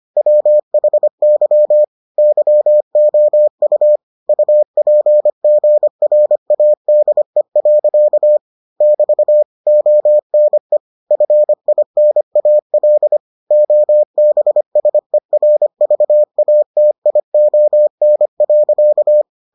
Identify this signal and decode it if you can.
Morse code, heard as a keyed tone, about 25 wpm, 600 Hz. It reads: WHY YOU UPGRADE. <BT> ONE FINAL OBSERVATION.